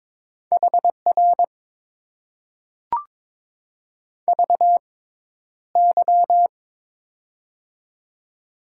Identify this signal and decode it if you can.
Morse code, heard as a keyed tone, about 22 wpm, spaced as for 11 wpm, 700 Hz. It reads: HR VY
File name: Top 90 QSO Elements - 6x Character Spacing - 22wpm